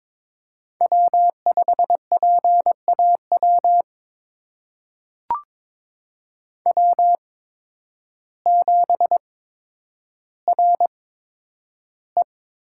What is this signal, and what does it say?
Morse code, heard as a keyed tone, about 22 wpm, spaced as for 10 wpm, 700 Hz. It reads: W5PAW W7RE
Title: USA Call Signs - 8x Character Spacing - 22wpm